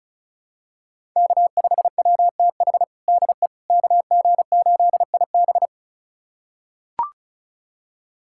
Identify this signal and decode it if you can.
Morse code, heard as a keyed tone, about 36 wpm, 700 Hz. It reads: K5WTH DE KG8IB
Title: USA Callsign DE Callsign - 5 Characters - 35wpm